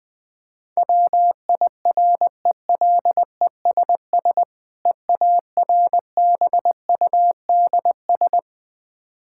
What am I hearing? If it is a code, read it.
Morse code, heard as a keyed tone, about 21 wpm, 700 Hz. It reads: WIRELESS EARBUDS
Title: Difficult Words with Hints - 20wpm